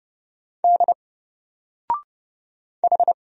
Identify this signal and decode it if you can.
Morse code, heard as a keyed tone, about 30 wpm, 700 Hz. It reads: D H
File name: Single-Letter-Number-30wpm